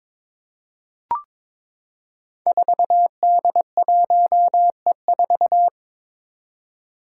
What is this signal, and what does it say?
Morse code, heard as a keyed tone, about 23 wpm, 700 Hz. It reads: 4D1E4
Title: Five Challenging Letter-Number Combinations 22wpm